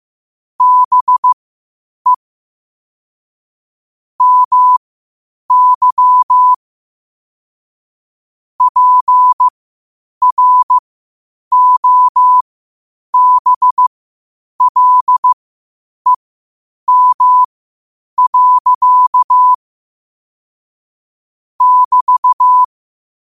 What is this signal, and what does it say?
Morse code, heard as a keyed tone, about 15 wpm, spaced as for 8 wpm, 1000 Hz. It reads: BE MY PROBLEM. <BT>
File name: qotdfiles / 8wpm